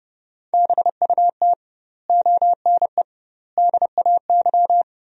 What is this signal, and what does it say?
Morse code, heard as a keyed tone, about 30 wpm, 700 Hz. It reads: BUT ONE DAY
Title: 2x Word-Spacing - Sentences from Top 400 Words - 30wpm